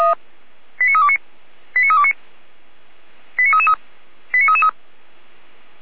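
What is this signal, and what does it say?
Five-tone selective call (ZVEI), heard as quick runs of five tones, tones of about 70 ms. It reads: 89329 89329 89333 89333
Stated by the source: Five-tone ZVEI selective call received by SDR, Feuerwehr Orlamünde.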